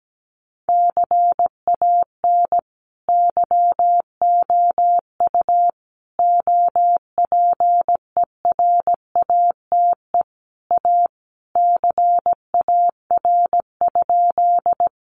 Morse code, heard as a keyed tone, about 17 wpm, 700 Hz. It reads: CAN YOU OPERATE A CAR?
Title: Sentences from Top 1500 Words 17wpm